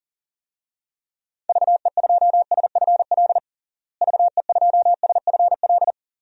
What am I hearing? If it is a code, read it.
Morse code, heard as a keyed tone, about 40 wpm, 700 Hz. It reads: VE2SFL VE2SFL
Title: DXCC-100 Call Signs - 6 Characters - Repeated 3x - 40wpm